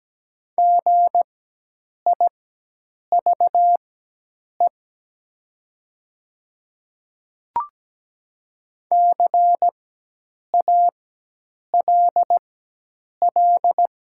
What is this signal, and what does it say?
Morse code, heard as a keyed tone, about 17 wpm, spaced as for 8 wpm, 700 Hz. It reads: GIVE CALL
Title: 4x Character Spacing - Top 200 Words - 17wpm